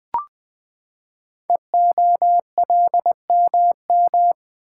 Morse code, heard as a keyed tone, about 20 wpm, 700 Hz. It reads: EOLMM
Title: Code Groups - 5 Letters - 20wpm